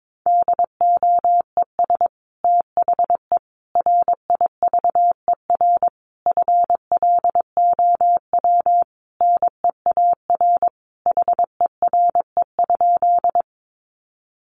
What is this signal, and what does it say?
Morse code, heard as a keyed tone, about 22 wpm, 700 Hz. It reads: DOES THE RIVER FLOW NEAR HERE?
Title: Sentences from Top 1200 Words 22wpm